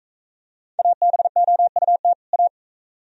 Morse code, heard as a keyed tone, about 41 wpm, 700 Hz. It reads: ABOUT A